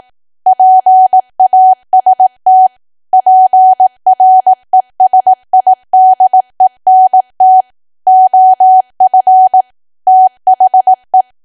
Morse code, heard as a keyed tone, about 18 wpm, 750 Hz. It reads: PAST PRESIDENT OF THE
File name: Archive / 18 WPM / 18WPM